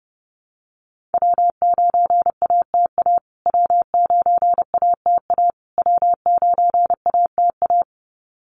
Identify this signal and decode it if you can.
Morse code, heard as a keyed tone, about 29 wpm, 700 Hz. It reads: W9ATA W9ATA W9ATA
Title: USA Call Signs Repeated 3x 30wpm